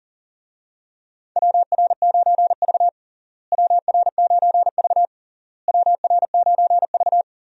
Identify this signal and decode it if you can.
Morse code, heard as a keyed tone, about 37 wpm, 700 Hz. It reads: WR9V WR9V WR9V
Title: USA Call Signs - 4 Characters - Repeated 3x - 40wpm